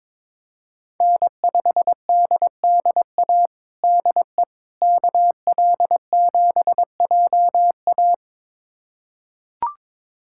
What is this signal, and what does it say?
Morse code, heard as a keyed tone, about 22 wpm, 700 Hz. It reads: N5DDA DE KL7JA
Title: USA Callsign DE Callsign - 5 Characters - 22wpm